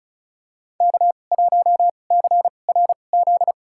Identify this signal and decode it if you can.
Morse code, heard as a keyed tone, about 36 wpm, spaced as for 25 wpm, 700 Hz. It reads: K1CRZ
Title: USA Call Signs - 5 Characters - 2x Character Spacing - 35wpm